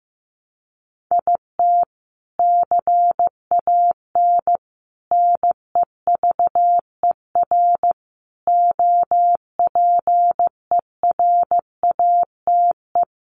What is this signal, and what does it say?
Morse code, heard as a keyed tone, about 15 wpm, 700 Hz. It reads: IT CAN NEVER OPERATE
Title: Sentences from Top 1500 Words 15wpm